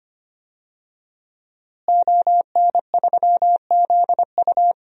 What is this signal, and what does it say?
Morse code, heard as a keyed tone, about 25 wpm, 700 Hz. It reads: ON3ZU
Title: DXCC-100 Call Signs - 5 Characters - Repeated 3x - 25wpm